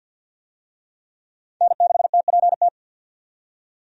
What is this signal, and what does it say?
Morse code, heard as a keyed tone, about 48 wpm, 700 Hz. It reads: N6TPT
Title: USA Call Signs - 5 Characters - 50wpm